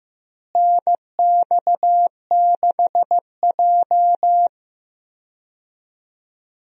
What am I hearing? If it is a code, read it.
Morse code, heard as a keyed tone, about 15 wpm, 700 Hz. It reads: NX6J
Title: USA Call Signs - 4 Characters - Repeated 3x - 15wpm